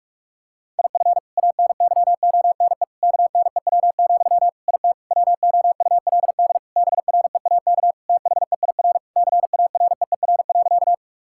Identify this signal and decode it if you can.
Morse code, heard as a keyed tone, about 46 wpm, 700 Hz. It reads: IF ANYONE KNEW, IT WOULD BREAK THEIR CAREER.
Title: Sentences from Top 1000 Words - Review Part 6 - 45wpm